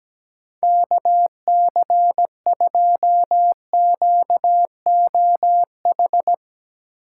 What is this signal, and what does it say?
Morse code, heard as a keyed tone, about 17 wpm, 700 Hz. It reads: KC2QOH
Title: Speed Racing - USA 2x3 Call Signs 17wpm